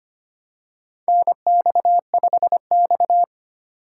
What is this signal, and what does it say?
Morse code, heard as a keyed tone, about 25 wpm, 690 Hz. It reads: NX5X
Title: USA Call Signs - 4 Characters - 4x Character Spacing - 25wpm